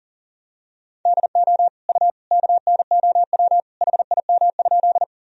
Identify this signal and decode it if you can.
Morse code, heard as a keyed tone, about 40 wpm, 700 Hz. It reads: DO U KNOW HIM?
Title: Sentences from Top 100 Words and Abbreviations 40wpm